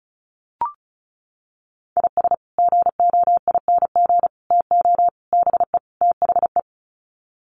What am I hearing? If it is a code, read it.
Morse code, heard as a keyed tone, about 34 wpm, 700 Hz. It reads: IS GOING TO BE THE
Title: Sets of 5 Words 35wpm